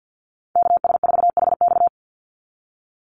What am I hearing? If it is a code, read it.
Morse code, heard as a keyed tone, about 50 wpm, 700 Hz. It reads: KS4HX
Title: USA Call Signs 50wpm